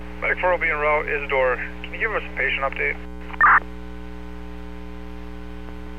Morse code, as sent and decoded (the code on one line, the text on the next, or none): none